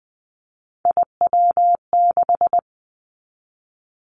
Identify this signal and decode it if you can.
Morse code, heard as a keyed tone, about 20 wpm, 700 Hz. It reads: IW6
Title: Three Challenging Letter-Number Combinations 20wpm